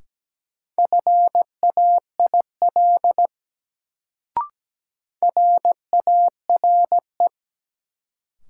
Morse code, heard as a keyed tone, about 17 wpm, 700 Hz. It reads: FAIL RARE
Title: Sentences from Top 1100 Words - New Word Drill - 17wpm